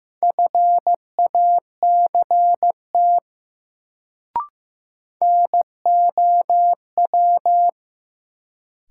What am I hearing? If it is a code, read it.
Morse code, heard as a keyed tone, about 15 wpm, 700 Hz. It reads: FACT NOW
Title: Sets of 2 Words - New Word Drill - 15wpm